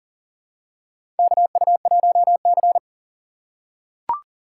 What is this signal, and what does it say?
Morse code, heard as a keyed tone, about 42 wpm, 690 Hz. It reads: KU1C
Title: USA Call Signs - 4 Characters - 6x Character Spacing - 40wpm